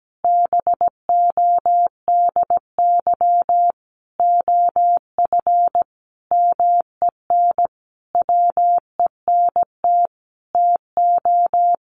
Morse code, heard as a keyed tone, about 17 wpm, 700 Hz. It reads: BODY OF MEN WENT TO
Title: Sentences from Top 1300 Words 17wpm